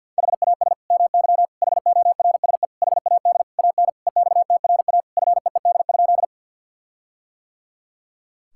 Morse code, heard as a keyed tone, about 50 wpm, 700 Hz. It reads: HAS MY HORSE HAD AN EXTRA FEED?
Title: Sentences from Top 1600 Words 50wpm